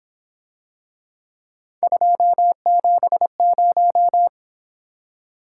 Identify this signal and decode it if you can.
Morse code, heard as a keyed tone, about 28 wpm, 700 Hz. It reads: 270
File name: Speed-Racing - Three Letter-Number Combinations - 22wpm